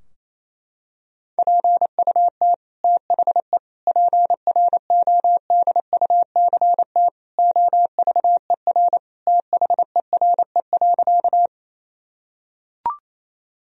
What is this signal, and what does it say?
Morse code, heard as a keyed tone, about 28 wpm, 700 Hz. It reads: PUT THE PRODUCT OVER THERE.
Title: Sentences from Top 1000 Words - Review Part 2 - 28wpm